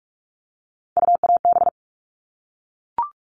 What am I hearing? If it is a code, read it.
Morse code, heard as a keyed tone, about 44 wpm, 700 Hz. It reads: UAB
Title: All Cut Numbers - 3-Digit 45wpm